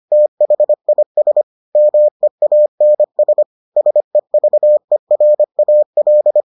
Morse code, heard as a keyed tone, about 25 wpm, 600 Hz. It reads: THIS MEANS SEVERAL